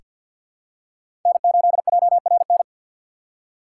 Morse code, heard as a keyed tone, about 47 wpm, 700 Hz. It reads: N8JRN
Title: USA Call Signs - 5 Characters - 2x Character Spacing - 50wpm